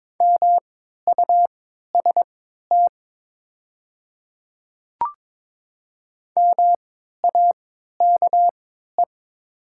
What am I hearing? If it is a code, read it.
Morse code, heard as a keyed tone, about 22 wpm, spaced as for 13 wpm, 700 Hz. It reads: MUST MAKE
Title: 3x Character Spacing - Top 200 Words - 22wpm